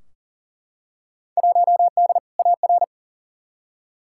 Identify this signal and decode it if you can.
Morse code, heard as a keyed tone, about 40 wpm, 700 Hz. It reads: 1D AR
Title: Field Day Exchanges - 40wpm